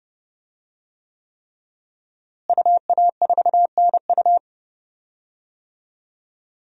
Morse code, heard as a keyed tone, about 32 wpm, 700 Hz. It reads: UA4NU
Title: DXCC-100 Call Signs - 5 Characters - Repeated 3x - 30wpm